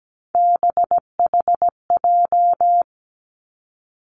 Morse code, heard as a keyed tone, about 17 wpm, 700 Hz. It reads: BHJ
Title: Three Challenging Letter-Number Combinations 17wpm